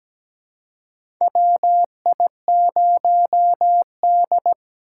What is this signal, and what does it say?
Morse code, heard as a keyed tone, about 17 wpm, 700 Hz. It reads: WI0D
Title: USA Call Signs - 4 Characters - 3x Character Spacing - 17wpm